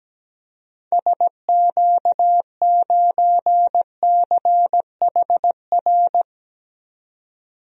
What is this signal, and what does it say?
Morse code, heard as a keyed tone, about 17 wpm, 700 Hz. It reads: SQ9CHR
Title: DXCC-100 Call Signs - 6 Characters - 17wpm